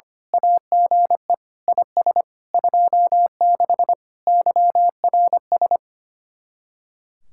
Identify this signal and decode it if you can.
Morse code, heard as a keyed tone, about 24 wpm, 700 Hz. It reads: AGE IS 26 YRS
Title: Speed-Racing Common QSO Phrases - 25wpm